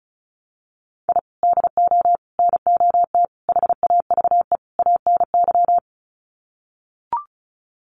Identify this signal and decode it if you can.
Morse code, heard as a keyed tone, about 34 wpm, 700 Hz. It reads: I DO NOT HAVE ANY
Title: Sets of 5 Words 35wpm